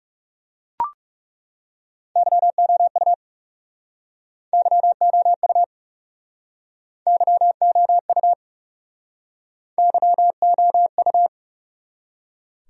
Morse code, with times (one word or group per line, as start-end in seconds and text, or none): 2.2-3.1 YOU
4.5-5.6 YOU
7.1-8.3 YOU
9.8-11.3 YOU